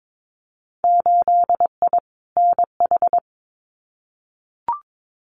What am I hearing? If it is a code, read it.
Morse code, heard as a keyed tone, about 22 wpm, 700 Hz. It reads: 8I NH
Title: Winter Field Day Exchanges Encore 22wpm